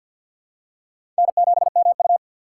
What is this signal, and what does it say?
Morse code, heard as a keyed tone, about 49 wpm, 700 Hz. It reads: N7MU
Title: USA Call Signs - 4 Characters - 6x Character Spacing - 50wpm